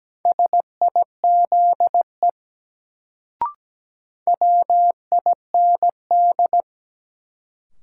Morse code, heard as a keyed tone, about 17 wpm, 700 Hz. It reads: SIZE WIND
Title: Sentences from Top 400 Words - New Word Drill - 17wpm